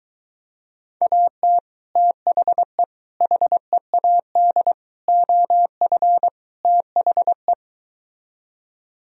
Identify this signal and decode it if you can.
Morse code, heard as a keyed tone, about 22 wpm, 700 Hz. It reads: AT THE HEAD OF THE